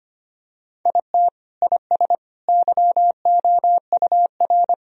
Morse code, heard as a keyed tone, about 25 wpm, 700 Hz. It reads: IT IS YOUR
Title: Sentences from Top 1000 Words - Review Part 4 - 25wpm